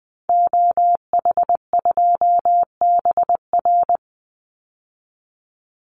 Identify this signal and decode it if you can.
Morse code, heard as a keyed tone, about 20 wpm, 700 Hz. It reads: OH2BR
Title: Call Signs - 20wpm